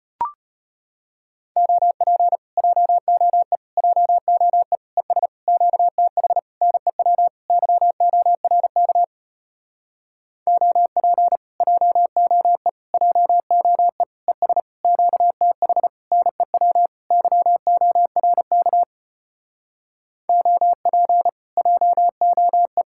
Morse code, heard as a keyed tone, about 33 wpm, 700 Hz. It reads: OP JOE JOE ES QTH NEW YORK OP JOE JOE ES QTH NEW YORK OP JOE